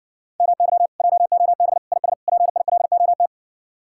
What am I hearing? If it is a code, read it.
Morse code, heard as a keyed tone, about 68 wpm, 700 Hz. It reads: MY JOB IS PILOT